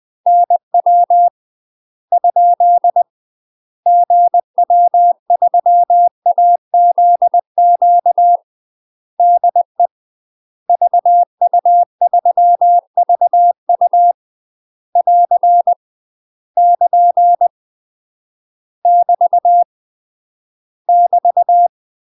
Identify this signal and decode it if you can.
Morse code, heard as a keyed tone, about 20 wpm, 700 Hz. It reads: NW ? GW3AZQ DE VU3VU <AR> <KN> <BT> <BT>
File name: UK MASTER QSOs TRAINING / 20WPM / UK MASTER QSOs 0039